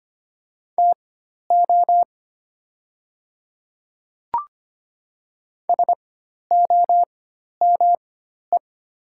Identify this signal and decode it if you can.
Morse code, heard as a keyed tone, about 25 wpm, spaced as for 12 wpm, 700 Hz. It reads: TO SOME
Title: 4x Character Spacing - Top 100 Words - 25wpm